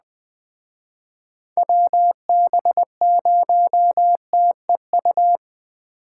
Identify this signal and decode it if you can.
Morse code, heard as a keyed tone, about 21 wpm, 700 Hz. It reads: WB0TEU